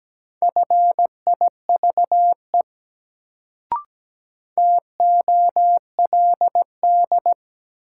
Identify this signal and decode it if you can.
Morse code, heard as a keyed tone, about 17 wpm, 700 Hz. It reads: FIVE TOLD
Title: Sentences from Top 400 Words - New Word Drill - 17wpm